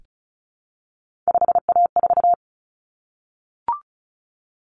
Morse code, heard as a keyed tone, about 34 wpm, 700 Hz. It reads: 5A4
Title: Common Cut Numbers - 3-Digit 35wpm